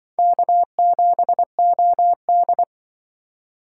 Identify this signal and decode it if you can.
Morse code, heard as a keyed tone, about 23 wpm, 700 Hz. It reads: K7OD
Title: Speed Racing - USA 1x2 Call Signs 20wpm